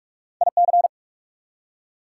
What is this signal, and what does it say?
Morse code, heard as a keyed tone, about 44 wpm, 690 Hz. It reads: IC
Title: Prefixes and Suffixes 45wpm